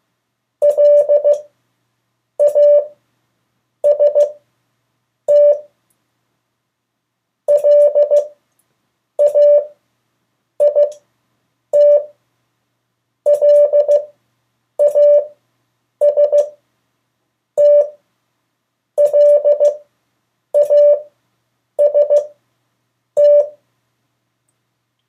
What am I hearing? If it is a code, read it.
Morse code, heard as a keyed tone, about 15 wpm, spaced as for 7 wpm, 580 Hz. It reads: LAST LAIT LAST LAST
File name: 3of4-words-2